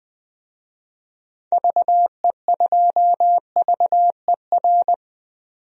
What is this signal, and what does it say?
Morse code, heard as a keyed tone, about 21 wpm, 700 Hz. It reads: VE2VER
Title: DXCC-100 Call Signs - 6 Characters - 20wpm